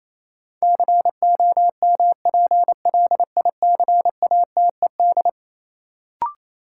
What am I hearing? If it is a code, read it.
Morse code, heard as a keyed tone, about 28 wpm, 700 Hz. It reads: COMPLICATED
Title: Sentences from Top 1200 Words - New Word Drill Alt - 28wpm